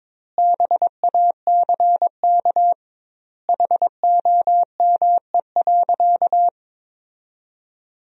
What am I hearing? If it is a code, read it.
Morse code, heard as a keyed tone, about 22 wpm, 700 Hz. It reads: BACK HOME.
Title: 2x Word-Spacing - Sentences from Top 500 Words - 22wpm